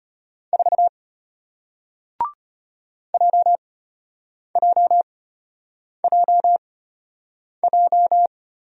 Morse code, heard as a keyed tone, about 34 wpm, spaced as for 8 wpm, 700 Hz. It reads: 4 JJJJ